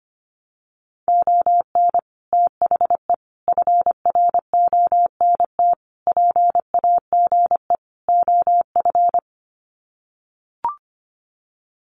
Morse code, heard as a keyed tone, about 25 wpm, 700 Hz. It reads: ON THE FRONT PAGE OF
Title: Sets of 5 Words Encore 25wpm